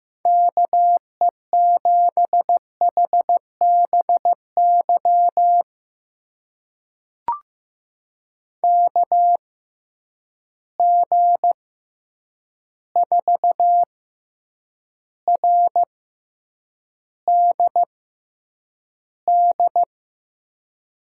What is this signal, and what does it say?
Morse code, heard as a keyed tone, about 15 wpm, spaced as for 8 wpm, 700 Hz. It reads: KE7HBY KG4RDD
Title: USA Call Signs - 6 Characters - 6x Character Spacing - 15wpm